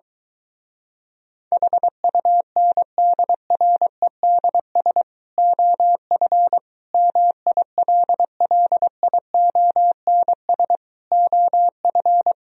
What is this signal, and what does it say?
Morse code, heard as a keyed tone, about 24 wpm, 700 Hz. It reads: HUNDREDS OF MILLIONS OF